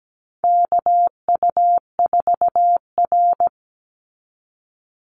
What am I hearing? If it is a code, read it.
Morse code, heard as a keyed tone, about 17 wpm, 700 Hz. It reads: KU4R
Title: NA SOTA Chaser Callsigns 17wpm